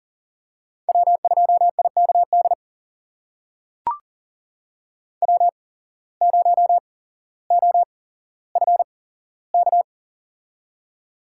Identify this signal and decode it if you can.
Morse code, heard as a keyed tone, about 40 wpm, spaced as for 17 wpm, 700 Hz. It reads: W2IKD W0OFK
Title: USA Call Signs - 5 Characters - 8x Character Spacing - 40wpm